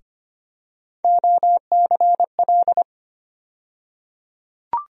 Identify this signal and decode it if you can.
Morse code, heard as a keyed tone, about 25 wpm, 700 Hz. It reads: OCL
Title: USA Call Sign Postfixes - 3 Letters 25wpm